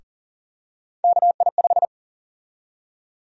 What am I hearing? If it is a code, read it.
Morse code, heard as a keyed tone, about 40 wpm, 700 Hz. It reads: KI5
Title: USA Call Sign Prefixes - Repeated 2x - 40wpm